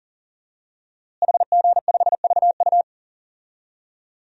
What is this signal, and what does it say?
Morse code, heard as a keyed tone, about 40 wpm, 700 Hz. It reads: HG5VU
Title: DXCC-100 Call Signs - 5 Characters - Repeated 3x - 40wpm